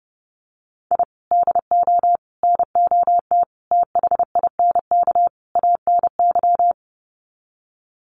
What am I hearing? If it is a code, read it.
Morse code, heard as a keyed tone, about 30 wpm, 700 Hz. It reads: I DO NOT THINK ANY
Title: Sets of 5 Words Encore 30wpm